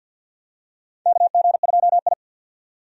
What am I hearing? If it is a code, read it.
Morse code, heard as a keyed tone, about 51 wpm, 700 Hz. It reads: KG2I
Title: USA Call Signs - 4 Characters - 50wpm